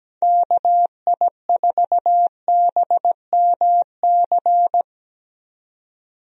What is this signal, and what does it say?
Morse code, heard as a keyed tone, about 17 wpm, 700 Hz. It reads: KI4BMC
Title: USA Call Signs - 6 Characters - Repeated 3x - 17wpm